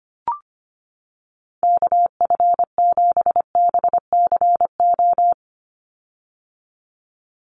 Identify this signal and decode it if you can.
Morse code, heard as a keyed tone, about 25 wpm, 700 Hz. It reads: KF7BCO
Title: USA Call Signs 25wpm